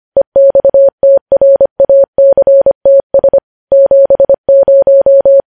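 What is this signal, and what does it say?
Morse code, heard as a keyed tone, about 25 wpm, 550 Hz. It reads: EXTRACTS 70
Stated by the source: Morse code news headlines sent at 25 words per minute.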